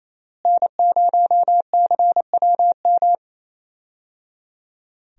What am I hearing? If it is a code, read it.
Morse code, heard as a keyed tone, about 27 wpm, 700 Hz. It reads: N0CWM
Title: USA Call Signs - 5 Characters - Repeated 3x - 28wpm